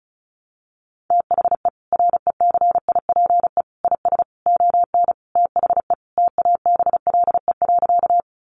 Morse code, heard as a keyed tone, about 34 wpm, 700 Hz. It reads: THE RECIPE IS ON THE TABLE.
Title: Sentences from Top 1200 Words 35wpm